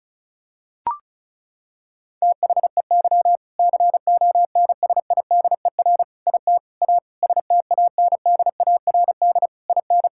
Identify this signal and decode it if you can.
Morse code, heard as a keyed tone, about 36 wpm, 700 Hz. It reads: THEY CONSIDER IT A STANDARD IN
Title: Sentences from Top 1000 Words - Review Part 6 - 35wpm